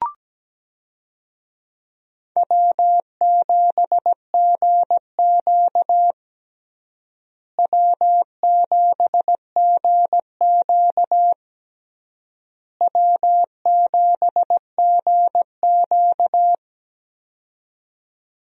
Morse code, heard as a keyed tone, about 17 wpm, 700 Hz. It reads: W7GQ W7GQ W7GQ
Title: USA Call Signs - 4 Characters - Repeated 3x - 17wpm